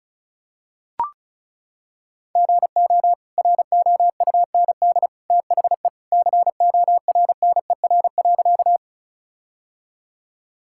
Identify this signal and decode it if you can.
Morse code, heard as a keyed tone, about 34 wpm, 700 Hz. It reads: GO ROUND THE CORNER.
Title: Sentences from Top 1600 Words 35wpm